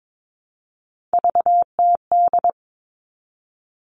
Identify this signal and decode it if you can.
Morse code, heard as a keyed tone, about 22 wpm, 700 Hz. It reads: VTD
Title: All Cut Numbers - 3-Digit 22wpm